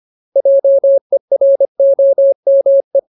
Morse code, heard as a keyed tone, about 25 wpm, 550 Hz. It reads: JEROME
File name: Ultimate Word List / 25WPM Ultimate